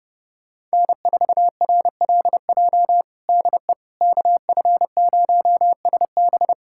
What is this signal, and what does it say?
Morse code, heard as a keyed tone, about 30 wpm, 700 Hz. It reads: N4RLJ DE KF0SB
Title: USA Callsign DE Callsign - 5 Characters - 30wpm